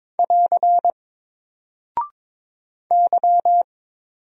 Morse code, heard as a keyed tone, about 22 wpm, 700 Hz. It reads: <AR> Y
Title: Single Character (Letter:Number:Punctuation:Prosign) 22wpm